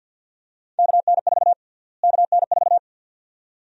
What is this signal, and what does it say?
Morse code, heard as a keyed tone, about 48 wpm, 700 Hz. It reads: KN4 KN4
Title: USA Call Sign Prefixes - Repeated 2x - 50wpm